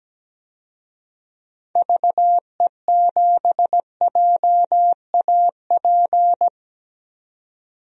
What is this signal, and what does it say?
Morse code, heard as a keyed tone, about 17 wpm, 700 Hz. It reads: VE7JAP
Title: DXCC-100 Call Signs - 6 Characters - 17wpm